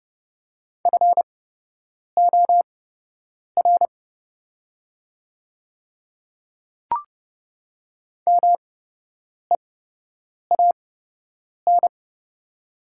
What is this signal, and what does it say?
Morse code, heard as a keyed tone, about 31 wpm, spaced as for 8 wpm, 700 Hz. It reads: FOR MEAN
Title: 8x Character Spacing - Top 200 Words - 30wpm